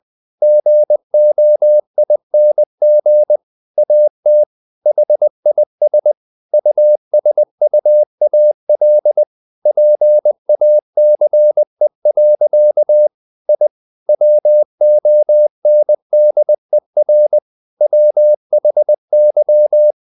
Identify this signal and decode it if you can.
Morse code, heard as a keyed tone, about 20 wpm, 600 Hz. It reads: GOING AT HIS USUAL PACE. I WONDER WHY
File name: the old man and the sea / the old man and the sea part 1 20wpm